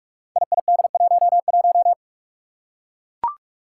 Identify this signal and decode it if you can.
Morse code, heard as a keyed tone, about 44 wpm, 700 Hz. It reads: IID11
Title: Five Challenging Letter-Number Combinations 45wpm